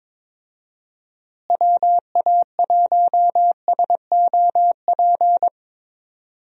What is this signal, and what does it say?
Morse code, heard as a keyed tone, about 22 wpm, 700 Hz. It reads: WA1SOP
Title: USA Call Signs - 6 Characters - Repeated 2x - 22wpm